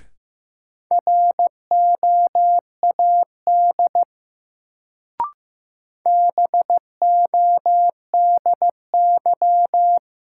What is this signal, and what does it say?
Morse code, heard as a keyed tone, about 15 wpm, 700 Hz. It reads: ROAD BODY
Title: Sentences from Top 400 Words - New Word Drill - 15wpm